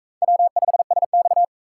Morse code, heard as a keyed tone, about 39 wpm, 700 Hz. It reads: W5SX